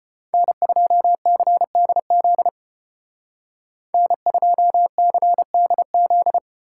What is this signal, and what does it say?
Morse code, heard as a keyed tone, about 31 wpm, 700 Hz. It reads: N2CDZ N2CDZ
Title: Speed Racing - USA 1x3 Call Signs 25wpm